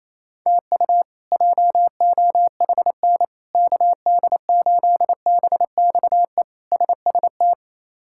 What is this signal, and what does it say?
Morse code, heard as a keyed tone, about 28 wpm, 700 Hz. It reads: TU JOHN KD8BXE SST
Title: K1USN SST QSO Practice - 28wpm